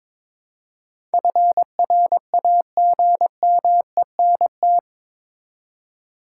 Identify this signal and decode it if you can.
Morse code, heard as a keyed tone, about 22 wpm, 700 Hz. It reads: FRAGMENT
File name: practice / 22wpm / Eight Letter Words - 22wpm